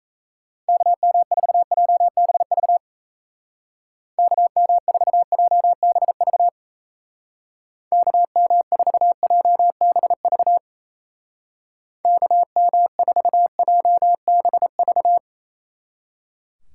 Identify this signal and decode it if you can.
Morse code, heard as a keyed tone, about 37 wpm, 700 Hz. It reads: KM4JBV KM4JBV KM4JBV KM4JBV